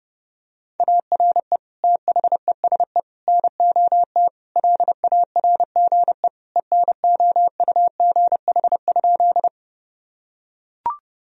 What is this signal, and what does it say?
Morse code, heard as a keyed tone, about 32 wpm, 700 Hz. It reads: ARE THESE NOT LARGE ENOUGH?
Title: Sentences from Top 1000 Words - Review Part 1 - 30wpm